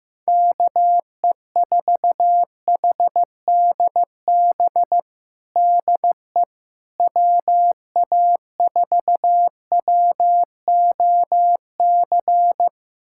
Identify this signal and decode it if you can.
Morse code, heard as a keyed tone, about 15 wpm, 700 Hz. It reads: KE4HDB DE WA4WOC
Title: USA Callsign DE Callsign - 6 Characters - 15wpm